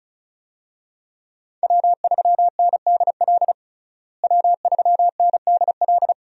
Morse code, heard as a keyed tone, about 36 wpm, 700 Hz. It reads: W3NDL W3NDL
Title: USA Call Signs - 5 Characters - Repeated 3x - 35wpm